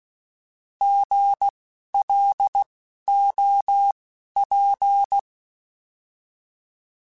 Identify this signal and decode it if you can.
Morse code, heard as a keyed tone, about 16 wpm, spaced as for 11 wpm, 800 Hz. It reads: GLOP
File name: Alphabetical / Alphabetical mp3s / Gs @10